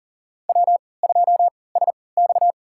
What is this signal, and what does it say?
Morse code, heard as a keyed tone, about 37 wpm, spaced as for 22 wpm, 700 Hz. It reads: W2SX
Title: USA Call Signs - 4 Characters - 3x Character Spacing - 40wpm